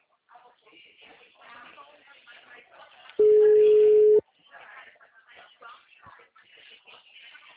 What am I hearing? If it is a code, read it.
Ringback tone (Europe, 425 Hz), with speech.
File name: out- -2011- - - .12997